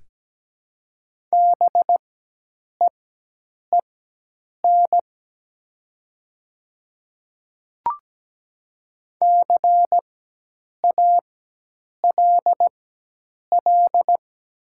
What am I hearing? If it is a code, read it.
Morse code, heard as a keyed tone, about 17 wpm, spaced as for 8 wpm, 700 Hz. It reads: BEEN CALL
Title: 4x Character Spacing - Top 100 Words - 17wpm